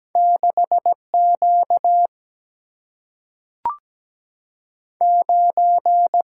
Morse code, heard as a keyed tone, about 17 wpm, 700 Hz. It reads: 6Q 9
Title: Single-Letter Single-Number 17wpm